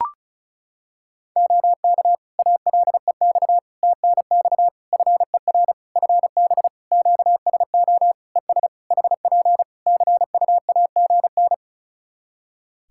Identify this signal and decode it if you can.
Morse code, heard as a keyed tone, about 36 wpm, 700 Hz. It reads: OK ALEX TNX FER FB QSO ES HP CUAGN
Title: Practice QSO - Exchange 4 - Ending - 35wpm